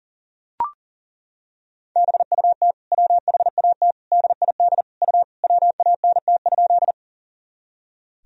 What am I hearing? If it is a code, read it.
Morse code, heard as a keyed tone, about 40 wpm, 700 Hz. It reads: BUT WHAT DID U WANT?
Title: Sentences from Top 200 Words and Abbreviations 40wpm